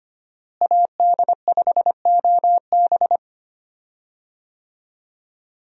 Morse code, heard as a keyed tone, about 25 wpm, 690 Hz. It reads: AD5OB
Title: USA Call Signs - 5 Characters - Repeated 2x - 25wpm